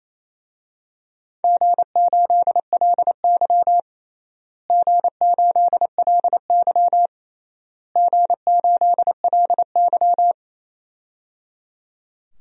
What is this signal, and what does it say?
Morse code, heard as a keyed tone, about 28 wpm, 700 Hz. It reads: G8LY G8LY G8LY
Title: DXCC-100 Call Signs - 4 Characters - Repeated 3x - 28wpm